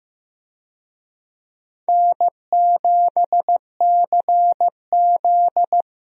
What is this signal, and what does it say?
Morse code, heard as a keyed tone, about 15 wpm, 700 Hz. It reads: N7CZ
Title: USA Call Signs - 4 Characters - Repeated 3x - 15wpm